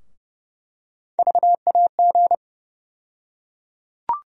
Morse code, heard as a keyed tone, about 30 wpm, 700 Hz. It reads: VAG
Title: USA Call Sign Postfixes - 3 Letters 30wpm